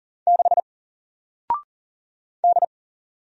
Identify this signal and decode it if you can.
Morse code, heard as a keyed tone, about 40 wpm, 700 Hz. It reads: 6 D
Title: Single Letter-Number Alt 40wpm